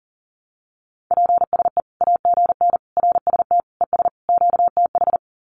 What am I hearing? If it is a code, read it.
Morse code, heard as a keyed tone, about 40 wpm, 700 Hz. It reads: PSE AGN RST ES QTH
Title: Common QSO Phrases 40wpm